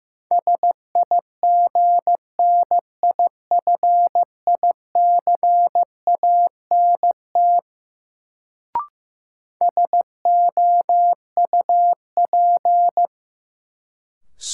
Morse code, heard as a keyed tone, about 15 wpm, 700 Hz. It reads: SIGNIFICANT SOUP
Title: Sentences from Top 1000 Words - Review Part 6 - New Word Drill - 15wpm